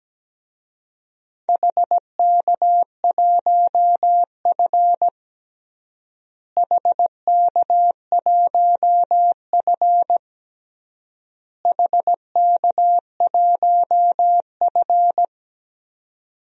Morse code, heard as a keyed tone, about 17 wpm, 700 Hz. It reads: HK1F HK1F HK1F
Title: DXCC-100 Call Signs - 4 Characters - Repeated 3x - 17wpm